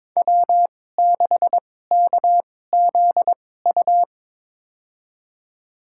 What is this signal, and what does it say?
Morse code, heard as a keyed tone, about 22 wpm, spaced as for 16 wpm, 700 Hz. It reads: W6KZU
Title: USA Call Signs - 5 Characters - 2x Character Spacing - 22wpm